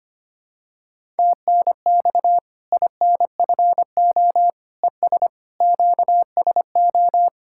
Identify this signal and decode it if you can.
Morse code, heard as a keyed tone, about 25 wpm, 700 Hz. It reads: TNX INFO ES QSO
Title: K1USN SST QSO Practice 25wpm